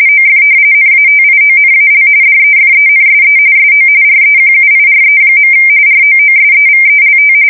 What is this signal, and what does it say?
RTTY, about 45.45 baud. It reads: OF NATIONAL DEFENSE SO REQUIRE. ROOSEVELT
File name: fsk170 / battleshipnj